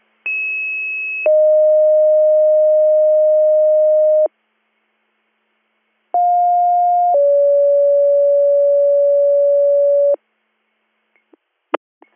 Two-tone page, heard as a one-second tone, then a three-second tone, about 2600 then 620 Hz, about 710 then 570 Hz. It is heard from 0.2 to 10.1 s.